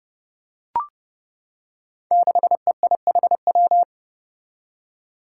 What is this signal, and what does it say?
Morse code, heard as a keyed tone, about 30 wpm, 700 Hz. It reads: 6EIHW
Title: Five Challenging Letter-Number Combinations 30wpm